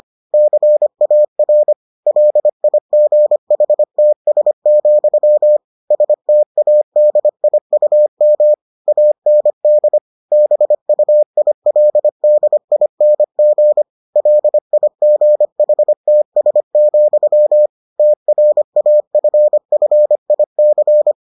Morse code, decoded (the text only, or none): CAR LIGHTS, STADIUM AND BUILDING LIGHTS, TRAFFIC